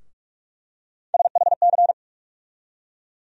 Morse code, heard as a keyed tone, about 46 wpm, 700 Hz. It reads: SHC
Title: USA Call Sign Postfixes - 3 Letters 45wpm